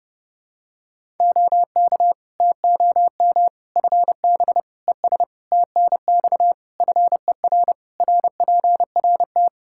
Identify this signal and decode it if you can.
Morse code, heard as a keyed tone, about 32 wpm, 700 Hz. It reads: OK TOM FB ES TNX FER RPRT
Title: Practice QSO - Exchange 2 - Info - 30wpm